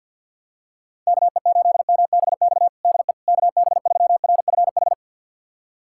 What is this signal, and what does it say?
Morse code, heard as a keyed tone, about 51 wpm, 700 Hz. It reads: KE8MBX DE KB3RFH
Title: USA Callsign DE Callsign 50wpm